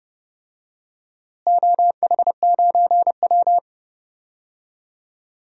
Morse code, heard as a keyed tone, about 30 wpm, 700 Hz. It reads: OH9W
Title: DXCC-100 Call Signs - 4 Characters - Repeated 2x - 30wpm